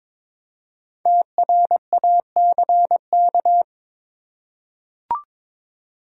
Morse code, heard as a keyed tone, about 22 wpm, 700 Hz. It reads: TRACK
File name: 5 Letter Words - First 3 Letters Given - 22wpm